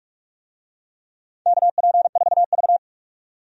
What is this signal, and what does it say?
Morse code, heard as a keyed tone, about 42 wpm, 700 Hz. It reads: KP4V
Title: DXCC-100 Call Signs - 4 Characters - 45wpm